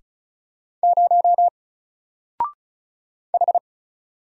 Morse code, heard as a keyed tone, about 36 wpm, 700 Hz. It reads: 0 H
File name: Single-Letter-Number-35wpm